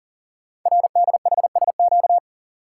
Morse code, heard as a keyed tone, about 40 wpm, 690 Hz. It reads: RDHSQ
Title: Code Groups - 5 Letters - 40wpm